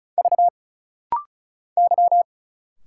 Morse code, heard as a keyed tone, about 33 wpm, 700 Hz. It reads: V Y
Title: Single Letters - Rapid-Fire 35wpm